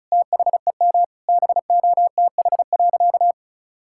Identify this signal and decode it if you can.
Morse code, heard as a keyed tone, about 36 wpm, 700 Hz. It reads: THEM BOTH.
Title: Sentences from Top 1600 Words 35wpm